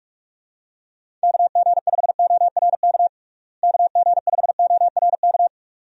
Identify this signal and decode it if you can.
Morse code, heard as a keyed tone, about 45 wpm, 700 Hz. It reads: KG5ORK KG5ORK
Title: USA Call Signs - 6 Characters - Repeated 2x - 45wpm